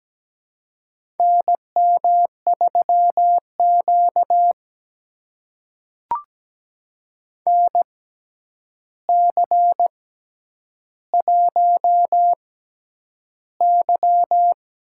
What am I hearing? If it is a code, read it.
Morse code, heard as a keyed tone, about 17 wpm, spaced as for 9 wpm, 700 Hz. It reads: NM3Q NC1Y
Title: USA Call Signs - 4 Characters - 6x Character Spacing - 17wpm